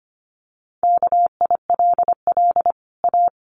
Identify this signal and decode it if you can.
Morse code, heard as a keyed tone, about 25 wpm, 700 Hz. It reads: KILL A
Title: Sentences from Top 1400 Words 25wpm